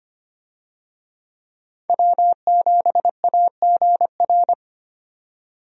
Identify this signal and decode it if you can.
Morse code, heard as a keyed tone, about 25 wpm, 700 Hz. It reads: W7AGR
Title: USA Call Signs - 5 Characters - 25wpm